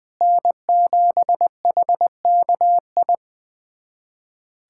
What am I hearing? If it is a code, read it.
Morse code, heard as a keyed tone, about 20 wpm, 700 Hz. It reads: N7HKI
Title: USA Callsign DE Callsign - 5 Characters - 20wpm